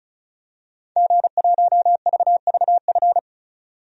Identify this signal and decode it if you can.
Morse code, heard as a keyed tone, about 34 wpm, 700 Hz. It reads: G1VVF
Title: DXCC-100 Call Signs - 5 Characters - 35wpm